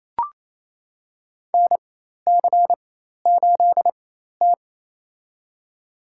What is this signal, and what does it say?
Morse code, heard as a keyed tone, about 28 wpm, spaced as for 13 wpm, 700 Hz. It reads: NC8T
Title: USA Call Signs - 4x Character Spacing - 28wpm